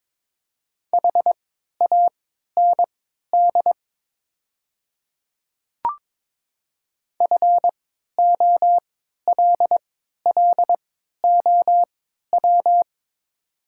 Morse code, heard as a keyed tone, about 22 wpm, spaced as for 13 wpm, 700 Hz. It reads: HAND FOLLOW
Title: 3x Character Spacing - Top 200 Words - 22wpm